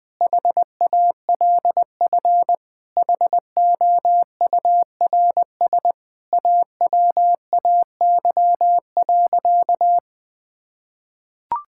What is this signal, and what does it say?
Morse code, heard as a keyed tone, about 20 wpm, 700 Hz. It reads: HALF HOURS AWAY.
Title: Sentences with 4-Word Phrases - Spoken Beginning - 20wpm